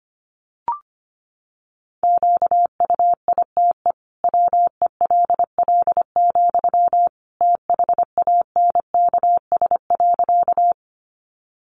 Morse code, heard as a keyed tone, about 25 wpm, 700 Hz. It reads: QUITE WELL, THANKS.
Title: Sentences from Top 700 Words 25wpm